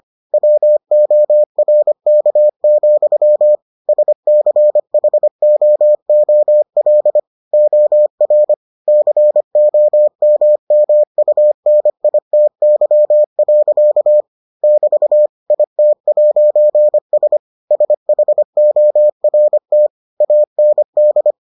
Morse code, heard as a keyed tone, about 25 wpm, 600 Hz. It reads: WORK, SCHOOL OR COMMUNITY. <BT> IT'S SHORT AND